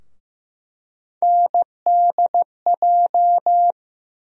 Morse code, heard as a keyed tone, about 15 wpm, 700 Hz. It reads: NDJ
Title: USA Call Sign Postfixes - 3 Letters 15wpm